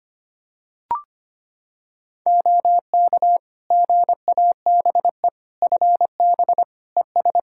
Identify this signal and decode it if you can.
Morse code, heard as a keyed tone, about 25 wpm, 700 Hz. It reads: OK GABE FB ES
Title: Practice QSO - Exchange 2 - Info - 25wpm